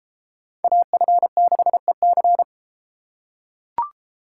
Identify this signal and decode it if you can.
Morse code, heard as a keyed tone, about 32 wpm, 700 Hz. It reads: AF6EC